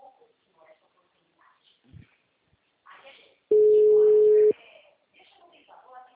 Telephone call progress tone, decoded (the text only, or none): ringback tone